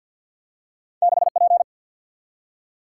Morse code, heard as a keyed tone, about 50 wpm, 700 Hz. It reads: 6 P
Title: Single-Letter Single-Number 50wpm